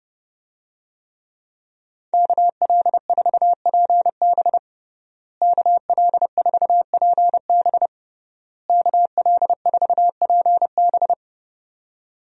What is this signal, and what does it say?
Morse code, heard as a keyed tone, about 32 wpm, 700 Hz. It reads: KL4PB KL4PB KL4PB
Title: USA Call Signs - 5 Characters - Repeated 3x - 30wpm